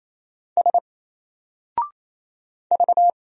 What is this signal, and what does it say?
Morse code, heard as a keyed tone, about 28 wpm, 700 Hz. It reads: S V
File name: Single-Letter-Number-28wpm